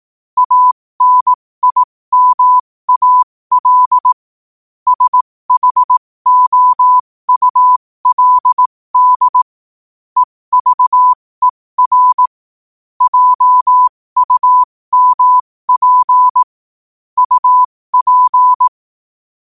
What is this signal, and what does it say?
Morse code, heard as a keyed tone, about 18 wpm, spaced as for 15 wpm, 1000 Hz. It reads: ANIMAL SHOULD EVER JUMP UP
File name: qotdfiles / 15wpm